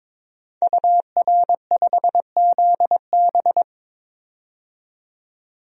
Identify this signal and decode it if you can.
Morse code, heard as a keyed tone, about 22 wpm, 700 Hz. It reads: UR5ZB
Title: DXCC-100 Call Signs - 5 Characters - 22wpm